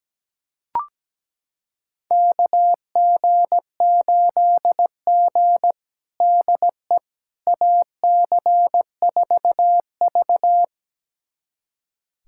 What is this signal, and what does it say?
Morse code, heard as a keyed tone, about 17 wpm, 700 Hz. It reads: KG8G DE AC4V
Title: USA Callsign DE Callsign - 4 Characters - 17wpm